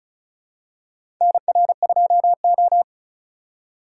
Morse code, heard as a keyed tone, about 36 wpm, 700 Hz. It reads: NR2O
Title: USA Call Signs - 4 Characters - 2x Character Spacing - 35wpm